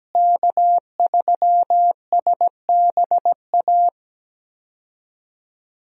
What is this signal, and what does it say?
Morse code, heard as a keyed tone, about 17 wpm, 700 Hz. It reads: K3SBA
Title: USA Call Signs - 5 Characters - Repeated 2x - 17wpm